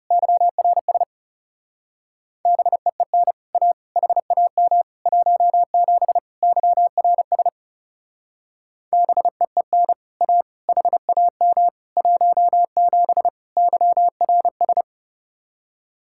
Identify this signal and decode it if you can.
Morse code, heard as a keyed tone, about 38 wpm, 700 Hz. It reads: YRS BEEN A HAM 17 YRS BEEN A HAM 17 YRS
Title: Speed-Racing Common QSO Phrases - 30wpm